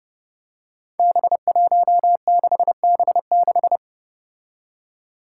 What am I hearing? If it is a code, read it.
Morse code, heard as a keyed tone, about 28 wpm, 700 Hz. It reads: B16B6
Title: Five Challenging Letter-Number Combinations 30wpm